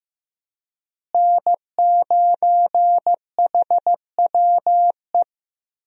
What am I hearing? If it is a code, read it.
Morse code, heard as a keyed tone, about 15 wpm, 700 Hz. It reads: N9HWE
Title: USA Call Signs - 5 Characters - Repeated 2x - 15wpm